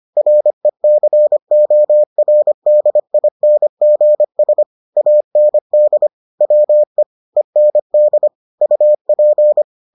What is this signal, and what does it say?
Morse code, heard as a keyed tone, about 25 wpm, 600 Hz. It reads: RECORDINGS AND WE END UP